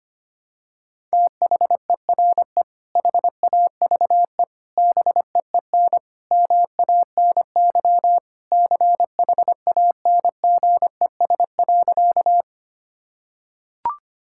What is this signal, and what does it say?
Morse code, heard as a keyed tone, about 25 wpm, 700 Hz. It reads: THERE HAVE BEEN MANY CHANGES.
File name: practice / 25wpm / Sentences with 3-Word Phrases - 25wpm